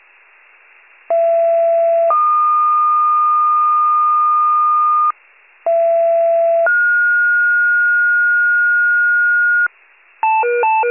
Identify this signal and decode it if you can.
Two-tone page, heard as a one-second tone, then a three-second tone, about 670 then 1220 Hz, about 670 then 1430 Hz, with noise. It is heard from 1.1 to 9.7 s.